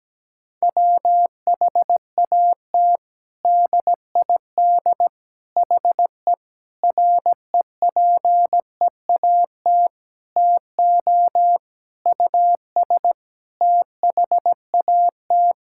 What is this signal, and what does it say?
Morse code, heard as a keyed tone, about 17 wpm, 700 Hz. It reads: WHAT DID HE REPEAT TO US THAT
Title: Sentences from Top 1600 Words 17wpm